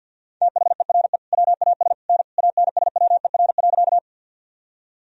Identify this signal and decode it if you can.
Morse code, heard as a keyed tone, about 52 wpm, 700 Hz. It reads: THERE WAS N ANSWER.
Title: Sentences from Top 300 Words and Abbreviations 50wpm